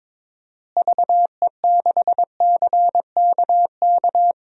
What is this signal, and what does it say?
Morse code, heard as a keyed tone, about 22 wpm, 700 Hz. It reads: VE6CKK
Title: DXCC-100 Call Signs - 6 Characters - Repeated 3x - 22wpm